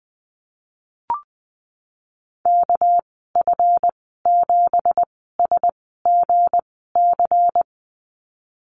Morse code, heard as a keyed tone, about 20 wpm, spaced as for 15 wpm, 700 Hz. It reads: KF7SGC
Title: 207 Lesson 33g - Letter j - Call Signs 20wpm